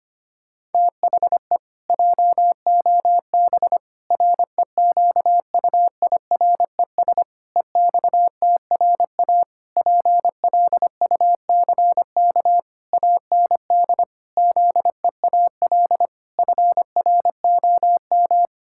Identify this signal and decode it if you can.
Morse code, heard as a keyed tone, about 25 wpm, 700 Hz. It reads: THE JOB REQUIRES EXTRA PLUCK AND ZEAL FROM
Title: Pangram Challenge v2 - 25wpm